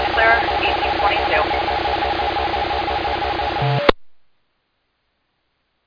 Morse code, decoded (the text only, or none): none